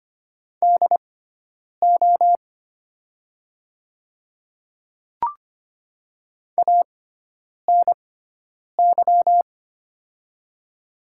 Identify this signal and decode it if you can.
Morse code, heard as a keyed tone, about 25 wpm, spaced as for 9 wpm, 700 Hz. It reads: DO ANY
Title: 6x Character Spacing - Top 300 Words - 25wpm